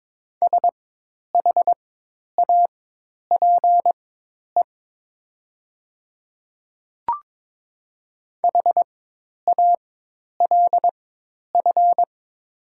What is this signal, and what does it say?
Morse code, heard as a keyed tone, about 22 wpm, spaced as for 10 wpm, 700 Hz. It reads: SHAPE HALF
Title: 4x Character Spacing - Top 500 Words - 22wpm